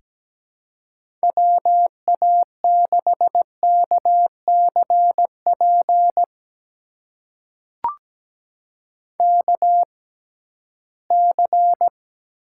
Morse code, heard as a keyed tone, about 17 wpm, spaced as for 13 wpm, 700 Hz. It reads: WA6KCP KC
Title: USA Call Signs - 6x Character Spacing - 17wpm